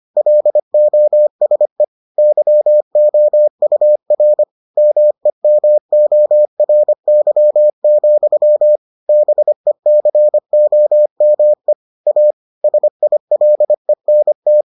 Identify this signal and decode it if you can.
Morse code, heard as a keyed tone, about 25 wpm, 600 Hz. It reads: LOSE YOUR MEMORY, BECOME A SILENT